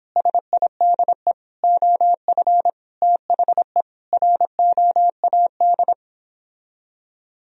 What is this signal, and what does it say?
Morse code, heard as a keyed tone, about 29 wpm, 700 Hz. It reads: SIDE OF THE ROAD